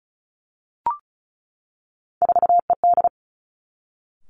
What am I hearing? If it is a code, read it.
Morse code, heard as a keyed tone, about 36 wpm, 700 Hz. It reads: 4ED
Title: Three Challenging Letter-Number Combinations 35wpm